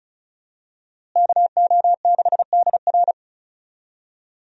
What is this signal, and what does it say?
Morse code, heard as a keyed tone, about 36 wpm, 700 Hz. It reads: KO6DR
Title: USA Call Signs - 5 Characters - Repeated 2x - 35wpm